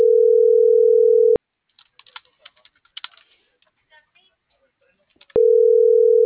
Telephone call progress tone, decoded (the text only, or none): ringback tone